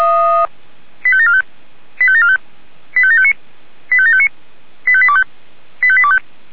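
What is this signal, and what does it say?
Five-tone selective call (ZVEI), heard as quick runs of five tones, tones of about 70 ms. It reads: 85745 85745 85759 85759 85825 85825
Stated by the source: ZVEI five-tone call received via SDR.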